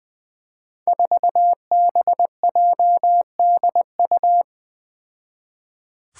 Morse code, heard as a keyed tone, about 20 wpm, 700 Hz. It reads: 4BJDU
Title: Five Challenging Letter-Number Combinations 20wpm